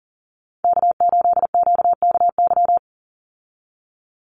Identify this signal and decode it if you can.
Morse code, heard as a keyed tone, about 40 wpm, 700 Hz. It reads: K8QKY
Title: Call Signs Repeated 2x - 40wpm